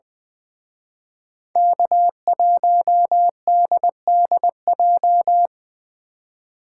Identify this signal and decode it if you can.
Morse code, heard as a keyed tone, about 23 wpm, 700 Hz. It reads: K1DDJ
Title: Speed Racing - USA 1x3 Call Signs 20wpm